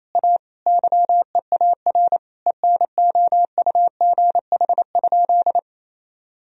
Morse code, heard as a keyed tone, about 28 wpm, 700 Hz. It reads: A YEAR ENOUGH?
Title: Sentences from Top 1000 Words - Review Part 2 - 28wpm